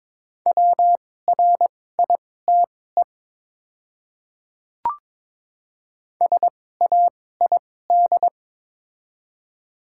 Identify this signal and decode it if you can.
Morse code, heard as a keyed tone, about 22 wpm, spaced as for 16 wpm, 700 Hz. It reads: WRITE SAID
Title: 2x Character Spacing - Top 100 Words - 22wpm